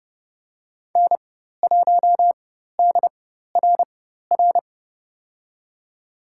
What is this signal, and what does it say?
Morse code, heard as a keyed tone, about 30 wpm, spaced as for 14 wpm, 700 Hz. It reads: N1DRR
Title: USA Call Signs - 5 Characters - 4x Character Spacing - 30wpm